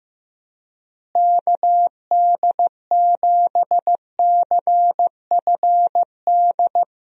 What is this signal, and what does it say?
Morse code, heard as a keyed tone, about 15 wpm, 700 Hz. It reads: KD7CFD
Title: USA Call Signs - 6 Characters - Repeated 3x - 15wpm